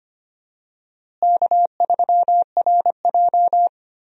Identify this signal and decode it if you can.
Morse code, heard as a keyed tone, about 25 wpm, 700 Hz. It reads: K3RJ
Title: USA Call Signs - 4 Characters - Repeated 3x - 25wpm